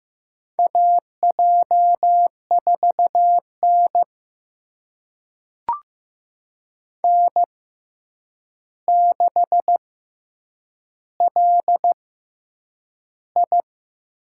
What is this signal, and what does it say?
Morse code, heard as a keyed tone, about 15 wpm, spaced as for 8 wpm, 700 Hz. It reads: AJ4N N6LI
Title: USA Call Signs - 4 Characters - 6x Character Spacing - 15wpm